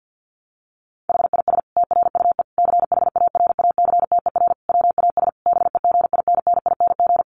Morse code, heard as a keyed tone, about 48 wpm, 700 Hz. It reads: HIS TRUE CHARACTER WAS BEGINNING
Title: Sentences from Top 1200 Words 50wpm